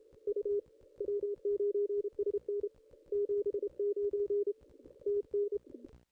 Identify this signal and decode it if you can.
Morse code, heard as a keyed tone, about 32 wpm, 440 Hz. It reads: U W9SN 79 TN
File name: contests / ss-cw-k5tr-op-n5rz -0400